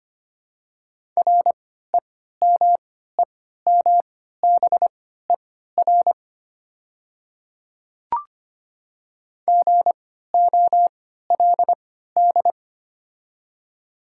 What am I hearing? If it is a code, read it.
Morse code, heard as a keyed tone, about 25 wpm, spaced as for 14 wpm, 700 Hz. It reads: REMEMBER GOLD
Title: 3x Character Spacing - Top 500 Words - 25wpm